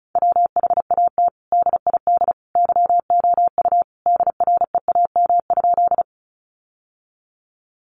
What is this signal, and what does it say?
Morse code, heard as a keyed tone, about 34 wpm, 700 Hz. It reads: WHAT DID YOU DREAM?
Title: Sentences from Top 1500 Words 35wpm